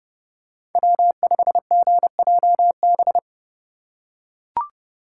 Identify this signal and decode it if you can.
Morse code, heard as a keyed tone, about 31 wpm, 700 Hz. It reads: W5GJB